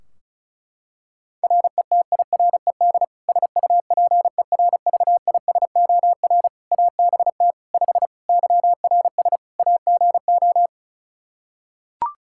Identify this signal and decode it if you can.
Morse code, heard as a keyed tone, about 34 wpm, 700 Hz. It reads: RETIRED SUPERVISOR ABT 5 YRS AGO
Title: Practice QSO - Exchange 3 - More Info - 35wpm